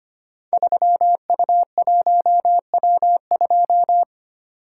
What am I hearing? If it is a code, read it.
Morse code, heard as a keyed tone, about 25 wpm, 700 Hz. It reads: 3U1W2
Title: Code Groups - 5 Letter:Numbers - 25wpm